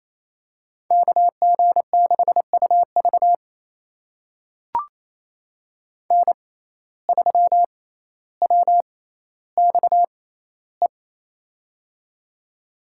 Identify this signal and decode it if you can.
Morse code, heard as a keyed tone, about 28 wpm, spaced as for 14 wpm, 700 Hz. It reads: KG6UV N3WXE
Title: USA Call Signs - 5 Characters - 6x Character Spacing - 28wpm